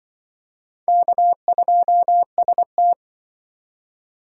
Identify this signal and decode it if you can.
Morse code, heard as a keyed tone, about 25 wpm, 700 Hz. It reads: K2ST
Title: Speed Racing - USA Call Signs 20wpm